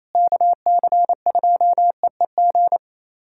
Morse code, heard as a keyed tone, about 28 wpm, 700 Hz. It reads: KC2EEG
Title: USA Callsign DE Callsign 28wpm